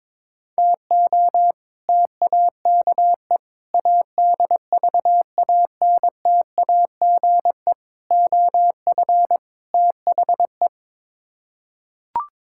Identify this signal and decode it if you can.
Morse code, heard as a keyed tone, about 22 wpm, 700 Hz. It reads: TO TAKE ADVANTAGE OF THE
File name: practice / 22wpm / Build A Phrase - 5-Word Phrases - 22wpm